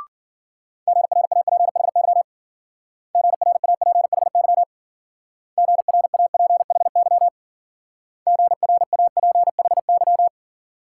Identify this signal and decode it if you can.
Morse code, heard as a keyed tone, about 49 wpm, 700 Hz. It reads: GRAPHY GRAPHY GRAPHY GRAPHY
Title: Speed-Racing Multi-letter Phonemes - 40wpm